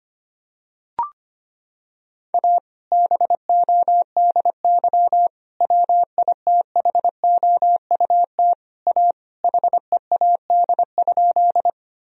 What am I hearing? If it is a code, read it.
Morse code, heard as a keyed tone, about 25 wpm, 700 Hz. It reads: A BODY WITHOUT A HEAD?
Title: Sentences from Top 1000 Words - Review Part 3 - 25wpm